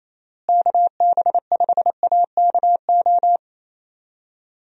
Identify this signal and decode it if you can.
Morse code, heard as a keyed tone, about 27 wpm, 700 Hz. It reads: KB5AKO
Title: USA Call Signs - 6 Characters - Repeated 2x - 28wpm